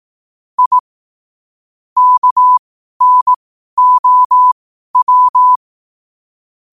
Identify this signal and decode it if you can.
Morse code, heard as a keyed tone, about 18 wpm, spaced as for 12 wpm, 1000 Hz. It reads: I KNOW
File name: qotdfiles / 12wpm